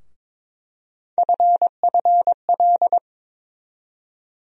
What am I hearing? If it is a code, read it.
Morse code, heard as a keyed tone, about 22 wpm, 700 Hz. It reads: FFL
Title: USA Call Sign Postfixes - 3 Letters 22wpm